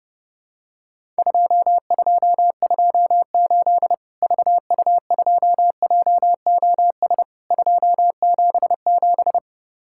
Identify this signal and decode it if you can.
Morse code, heard as a keyed tone, about 30 wpm, 700 Hz. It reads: 2228 VU2JOS 277
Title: Worked All Europe DX Contest - QTC Exchange - 30wpm